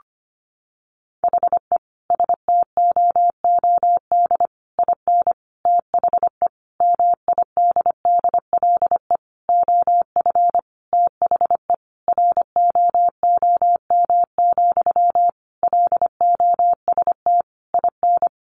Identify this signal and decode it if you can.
Morse code, heard as a keyed tone, about 25 wpm, 700 Hz. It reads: HE STOOD IN THE MIDDLE OF THE ROOM, LOST IN
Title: Sentences from Top 1400 Words 25wpm